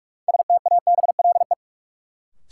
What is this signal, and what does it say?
Morse code, heard as a keyed tone, about 46 wpm, 700 Hz. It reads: STABLE
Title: Sentences from Top 1500 Words - New Word Drill Alt - 45wpm